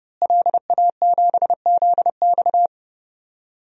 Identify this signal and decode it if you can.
Morse code, heard as a keyed tone, about 30 wpm, 700 Hz. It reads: LA7ZX
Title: DXCC-100 Call Signs - 5 Characters - Repeated 2x - 30wpm